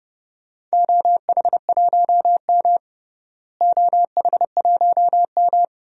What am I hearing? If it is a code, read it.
Morse code, heard as a keyed tone, about 30 wpm, 700 Hz. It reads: OH1M OH1M
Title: DXCC-100 Call Signs - 4 Characters - Repeated 3x - 30wpm